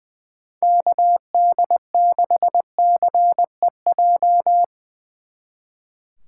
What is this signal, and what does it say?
Morse code, heard as a keyed tone, about 22 wpm, 700 Hz. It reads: KD6CEJ
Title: Speed Racing - USA 2x3 Call Signs 20wpm